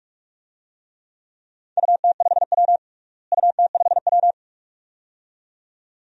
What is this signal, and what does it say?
Morse code, heard as a keyed tone, about 42 wpm, 700 Hz. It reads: UT5W UT5W
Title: DXCC-100 Call Signs - 4 Characters - Repeated 2x - 45wpm